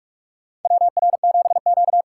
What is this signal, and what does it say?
Morse code, heard as a keyed tone, about 44 wpm, 690 Hz. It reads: WR7Q
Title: USA Call Signs - 4 Characters - 3x Character Spacing - 45wpm